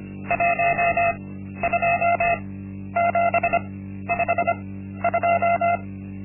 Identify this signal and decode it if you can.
Morse code, heard as a keyed tone, about 25 wpm, spaced as for 12 wpm, 690 Hz. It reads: 12752
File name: Linksmakalnis-1